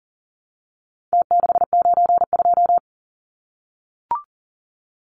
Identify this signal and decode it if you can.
Morse code, heard as a keyed tone, about 40 wpm, 700 Hz. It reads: T692
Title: Numbers - Contest Style - 4-Digit 40wpm